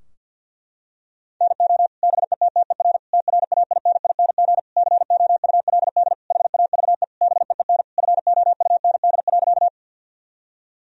Morse code, heard as a keyed tone, about 50 wpm, 700 Hz. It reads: NO BETTER TRAINING COULD HAVE BEEN FOUND.
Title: Sentences from Top 1000 Words - Review Part 4 - 50wpm